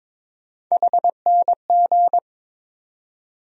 Morse code, heard as a keyed tone, about 22 wpm, 700 Hz. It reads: HNG
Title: USA Call Sign Postfixes - 3 Letters 22wpm